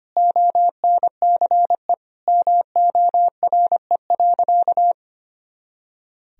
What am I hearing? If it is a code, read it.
Morse code, heard as a keyed tone, about 25 wpm, 700 Hz. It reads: ONCE MORE.
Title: Sentences from Top 500 Words and Abbreviations 25wpm